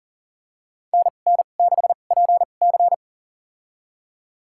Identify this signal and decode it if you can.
Morse code, heard as a keyed tone, about 40 wpm, spaced as for 29 wpm, 700 Hz. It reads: NN6PC
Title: USA Call Signs - 5 Characters - 2x Character Spacing - 40wpm